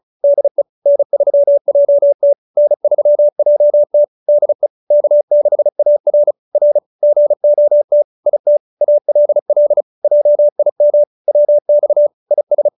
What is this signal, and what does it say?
Morse code, heard as a keyed tone, about 34 wpm, 590 Hz. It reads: DE N3JT N3JT DE K6AR R GOT IT ALL JIM WX IS